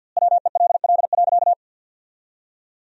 Morse code, heard as a keyed tone, about 47 wpm, 700 Hz. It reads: WELL.
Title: 2x Word-Spacing - Sentences from Top 500 Words - 50wpm